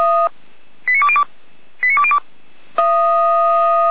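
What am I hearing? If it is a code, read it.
Five-tone selective call (ZVEI), heard as quick runs of five tones, tones of about 70 ms. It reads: 89202 89202
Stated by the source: Received by SDR, logged on a server of Feuerwehr Orlamünde (fire brigade).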